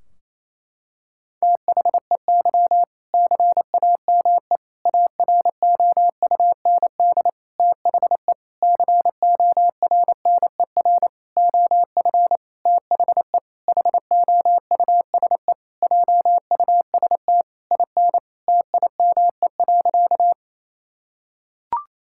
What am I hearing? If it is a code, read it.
Morse code, heard as a keyed tone, about 27 wpm, 700 Hz. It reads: THEY CAME AROUND THE CORNER OF THE HOUSE JUST IN TIME.
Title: Sentences from Top 1600 Words 28wpm